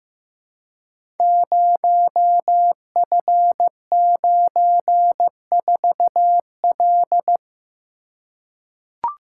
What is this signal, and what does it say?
Morse code, heard as a keyed tone, about 15 wpm, 700 Hz. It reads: 0F94L
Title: Code Groups - 5 Letter:Numbers - 15wpm